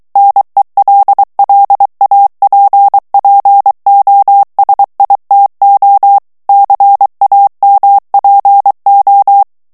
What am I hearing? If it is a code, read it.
Morse code, heard as a keyed tone, about 23 wpm, 800 Hz. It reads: NELLAPPOSITO CAMPO
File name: ITALIEN texte clair 2 vitesse 22 mots minute